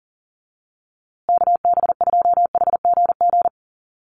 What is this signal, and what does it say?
Morse code, heard as a keyed tone, about 40 wpm, 700 Hz. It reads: KB2HGG
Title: USA Call Signs Repeated 2x 40wpm